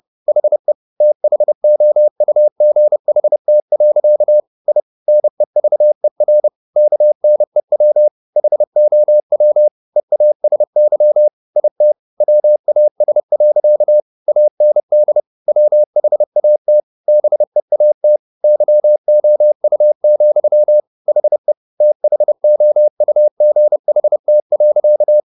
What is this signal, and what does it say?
Morse code, heard as a keyed tone, about 30 wpm, 600 Hz. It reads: HE THOUGHT. I NEVER KNEW HOW EASY IT WAS. AND WHAT BEAT YOU, HE THOUGHT.